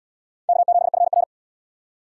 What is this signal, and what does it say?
Morse code, heard as a keyed tone, about 77 wpm, 700 Hz. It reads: K7FR